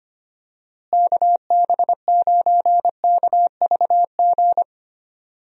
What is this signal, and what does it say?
Morse code, heard as a keyed tone, about 25 wpm, 700 Hz. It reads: KB9KVG
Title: USA Call Signs - 6 Characters - Repeated 3x - 25wpm